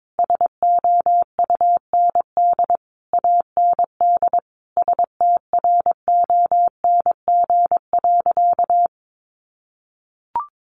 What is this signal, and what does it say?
Morse code, heard as a keyed tone, about 22 wpm, 700 Hz. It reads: SOUND AND STRONG.
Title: Sentences from Top 1400 Words 22wpm